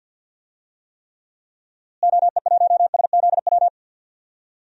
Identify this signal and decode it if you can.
Morse code, heard as a keyed tone, about 50 wpm, 700 Hz. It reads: OE1SZW
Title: DXCC-100 Call Signs - 6 Characters - Repeated 3x - 50wpm